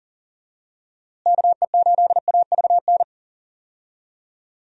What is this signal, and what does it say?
Morse code, heard as a keyed tone, about 40 wpm, 700 Hz. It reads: KE8AVN
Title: USA Call Signs - 6 Characters - Repeated 3x - 40wpm